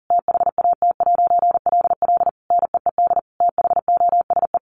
Morse code, heard as a keyed tone, about 40 wpm, 700 Hz. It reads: THAT'LL NEED THOSE
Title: A Sound Of Thunder 40wpm